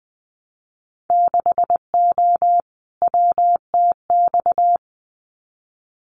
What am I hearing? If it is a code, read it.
Morse code, heard as a keyed tone, about 20 wpm, 700 Hz. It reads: 6O WTX
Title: Winter Field Day Exchanges Encore 20wpm